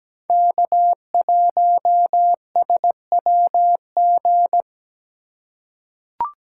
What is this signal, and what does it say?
Morse code, heard as a keyed tone, about 17 wpm, 700 Hz. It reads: K1SWG
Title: USA Call Signs - 5 Characters - 4x Character Spacing - 17wpm